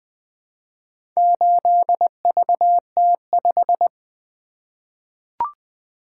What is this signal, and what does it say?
Morse code, heard as a keyed tone, about 20 wpm, 700 Hz. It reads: 8VT5
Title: Four Letter-Number Combinations - 20wpm